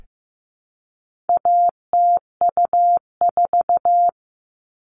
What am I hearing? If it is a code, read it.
Morse code, heard as a keyed tone, about 15 wpm, 700 Hz. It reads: ATU4
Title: All Cut Numbers - 4-Digit 15wpm